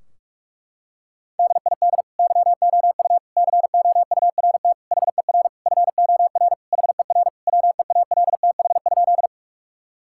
Morse code, heard as a keyed tone, about 46 wpm, 700 Hz. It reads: DID YOU COURT HER FOR HER WEALTH?
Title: Sentences from Top 1600 Words 45wpm